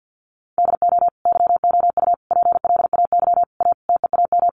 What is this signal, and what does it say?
Morse code, heard as a keyed tone, about 50 wpm, 700 Hz. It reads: DO YOU PLAY A TEAM
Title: Sentences from Top 1500 Words 50wpm